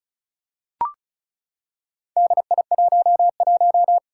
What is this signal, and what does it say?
Morse code, heard as a keyed tone, about 35 wpm, 700 Hz. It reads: DI11
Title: Four Challenging Letter-Number Combinations 35wpm